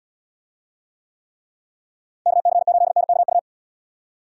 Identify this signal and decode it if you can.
Morse code, heard as a keyed tone, about 69 wpm, 700 Hz. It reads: KC8NKU